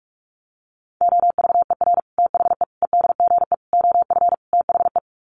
Morse code, heard as a keyed tone, about 44 wpm, 700 Hz. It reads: OVER THE EDGE OF THE
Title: Sets of 5 Words Encore2 45wpm